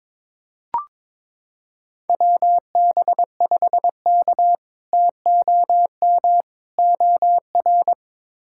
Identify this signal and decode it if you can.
Morse code, heard as a keyed tone, about 22 wpm, 700 Hz. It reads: WB5K TOM OR
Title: K1USN SST QSO Practice - 22wpm